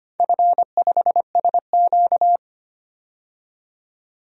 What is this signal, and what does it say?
Morse code, heard as a keyed tone, about 25 wpm, 700 Hz. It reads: F5SQ
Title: DXCC-100 Call Signs - 4 Characters - 25wpm